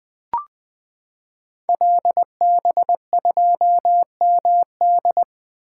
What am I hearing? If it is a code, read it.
Morse code, heard as a keyed tone, about 20 wpm, 700 Hz. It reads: LB2MD
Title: Code Groups - 5 Letter:Numbers - 20wpm